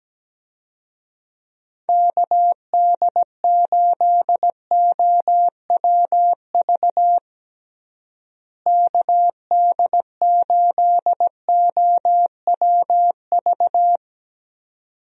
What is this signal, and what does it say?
Morse code, heard as a keyed tone, about 17 wpm, 700 Hz. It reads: KD8OWV KD8OWV
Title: USA Call Signs - 6 Characters - Repeated 2x - 17wpm